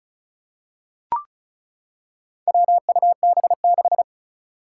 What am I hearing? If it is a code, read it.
Morse code, heard as a keyed tone, about 36 wpm, 700 Hz. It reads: WUB6
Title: Four Challenging Letter-Number Combinations 35wpm